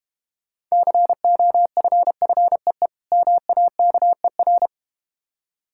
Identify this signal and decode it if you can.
Morse code, heard as a keyed tone, about 31 wpm, 700 Hz. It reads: COFFEE MAKER
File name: practice / 25wpm / Difficult Words - Speed-Racing with Hints - 25wpm